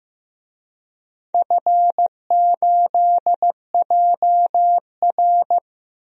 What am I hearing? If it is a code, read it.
Morse code, heard as a keyed tone, about 15 wpm, 700 Hz. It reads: F8JR
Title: DXCC-100 Call Signs - 4 Characters - Repeated 3x - 15wpm